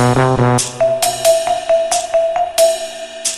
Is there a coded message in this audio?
No.